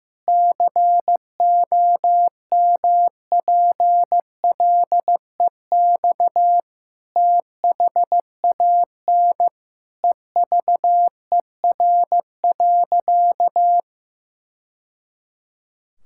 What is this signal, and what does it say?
Morse code, heard as a keyed tone, about 15 wpm, 700 Hz. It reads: COMPLEX THAN EVER.
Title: Sentences from Top 1000 Words - Review Part 6 - 15wpm